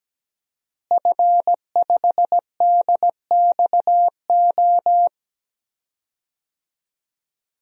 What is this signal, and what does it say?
Morse code, heard as a keyed tone, about 17 wpm, 700 Hz. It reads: F5DXO
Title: DXCC-100 Call Signs - 5 Characters - Repeated 3x - 17wpm